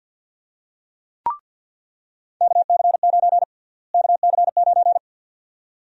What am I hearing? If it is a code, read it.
Morse code, heard as a keyed tone, about 48 wpm, 700 Hz. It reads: KC9 KC9
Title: USA Call Sign Prefixes - Repeated 2x - 50wpm